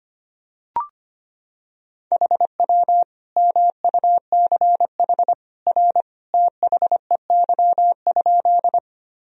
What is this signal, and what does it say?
Morse code, heard as a keyed tone, about 25 wpm, 700 Hz. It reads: HW MUCH R THEY?
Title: Sentences from Top 200 Words and Abbreviations 25wpm